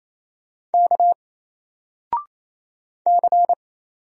Morse code, heard as a keyed tone, about 28 wpm, 700 Hz. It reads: K C
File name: Single-Letter-Number-28wpm